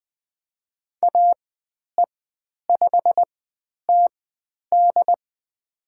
Morse code, heard as a keyed tone, about 20 wpm, spaced as for 10 wpm, 700 Hz. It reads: AE5TD
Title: 109 Lesson 18g - Letter m - Call Signs 20wpm